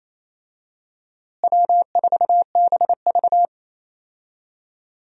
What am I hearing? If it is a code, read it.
Morse code, heard as a keyed tone, about 27 wpm, 700 Hz. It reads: W4BV
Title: USA Call Signs - 4 Characters - Repeated 2x - 28wpm